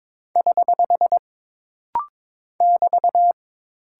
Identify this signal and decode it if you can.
Morse code, heard as a keyed tone, about 22 wpm, 700 Hz. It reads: <HH> <BT>
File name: practice / 22wpm / Single Punctuation-Prosign - Rapid-Fire - 22wpm